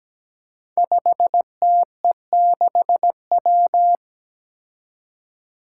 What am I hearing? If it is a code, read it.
Morse code, heard as a keyed tone, about 17 wpm, 700 Hz. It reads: 5TE6W
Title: Five Challenging Letter-Number Combinations 17wpm